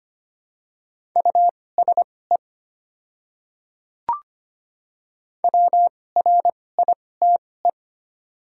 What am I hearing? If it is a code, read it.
Morse code, heard as a keyed tone, about 25 wpm, spaced as for 18 wpm, 690 Hz. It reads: USE WRITE
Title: 2x Character Spacing - Top 100 Words - 25wpm